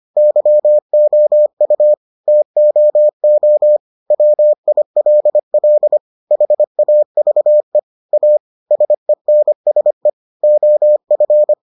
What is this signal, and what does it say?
Morse code, heard as a keyed tone, about 25 wpm, 600 Hz. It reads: YOU TOO WILL HAVE A SENSE OF